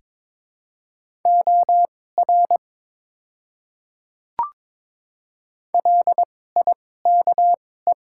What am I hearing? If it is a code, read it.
Morse code, heard as a keyed tone, about 22 wpm, spaced as for 16 wpm, 700 Hz. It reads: OR LIKE
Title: 2x Character Spacing - Top 200 Words - 22wpm